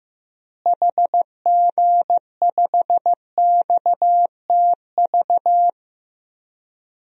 Morse code, heard as a keyed tone, about 15 wpm, 700 Hz. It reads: HG5XTV
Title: DXCC-100 Call Signs - 6 Characters - Repeated 2x - 15wpm